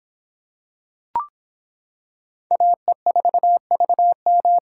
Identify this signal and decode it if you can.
Morse code, heard as a keyed tone, about 25 wpm, 700 Hz. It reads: AE4VM